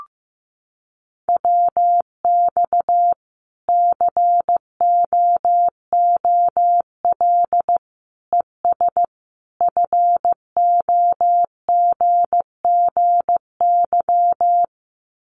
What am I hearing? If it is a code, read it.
Morse code, heard as a keyed tone, about 15 wpm, 700 Hz. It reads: WX COOL ES FOGGY
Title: Common QSO Phrases 15wpm